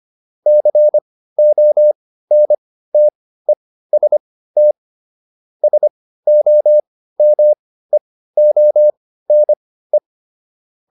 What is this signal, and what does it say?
Morse code, heard as a keyed tone, about 25 wpm, spaced as for 15 wpm, 600 Hz. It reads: CONTEST SOMEONE